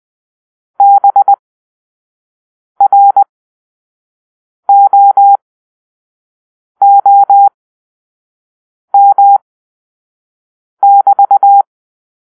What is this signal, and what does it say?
Morse code, heard as a keyed tone, about 20 wpm, spaced as for 5 wpm, 800 Hz. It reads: BROOM<BT>